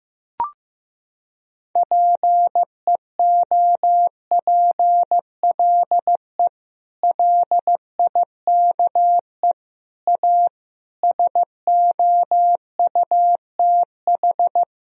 Morse code, heard as a keyed tone, about 15 wpm, 700 Hz. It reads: PEOPLE LIKE A SOUTH
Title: Sentences from Top 1000 Words - Review Part 3 - 15wpm